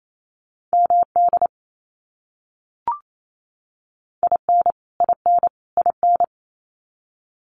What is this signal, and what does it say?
Morse code, heard as a keyed tone, about 28 wpm, 700 Hz. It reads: MD IN IN IN
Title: US States Repeated 3x 28wpm